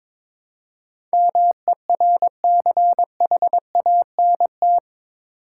Morse code, heard as a keyed tone, about 22 wpm, 700 Hz. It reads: MERCHANT
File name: practice / 22wpm / Eight Letter Words - 22wpm